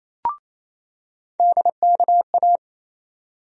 Morse code, heard as a keyed tone, about 28 wpm, 700 Hz. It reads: DKA
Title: USA Call Sign Postfixes - 3 Letters 28wpm